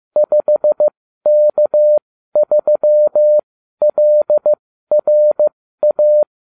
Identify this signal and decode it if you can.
Morse code, heard as a keyed tone, about 15 wpm, spaced as for 12 wpm, 600 Hz. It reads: 5K3LRA